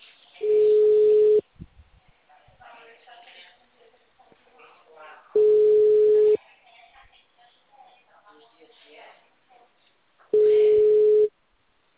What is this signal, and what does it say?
Ringback tone (Europe, 425 Hz), with speech.